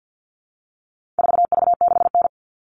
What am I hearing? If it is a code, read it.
Morse code, heard as a keyed tone, about 50 wpm, 700 Hz. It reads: 4V6N
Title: All Cut Numbers - 4-Digit 50wpm